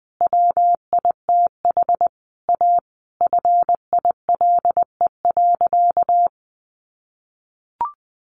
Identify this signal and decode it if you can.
Morse code, heard as a keyed tone, about 20 wpm, 700 Hz. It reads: WITH A FILE.
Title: Sentences from Top 1200 Words 20wpm